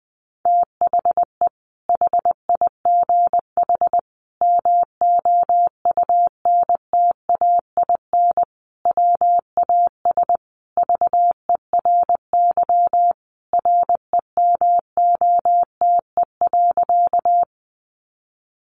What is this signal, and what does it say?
Morse code, heard as a keyed tone, about 20 wpm, 700 Hz. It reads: THE HIGH MOUNTAIN WAS VERY REMOTE.
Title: Sentences from Top 1300 Words 20wpm